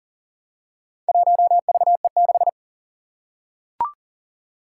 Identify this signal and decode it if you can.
Morse code, heard as a keyed tone, about 40 wpm, 700 Hz. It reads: 1VE6
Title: Four Challenging Letter-Number Combinations 40wpm